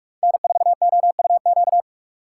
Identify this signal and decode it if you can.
Morse code, heard as a keyed tone, about 47 wpm, 700 Hz. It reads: N4OUQ
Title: USA Call Signs - 5 Characters - 6x Character Spacing - 45wpm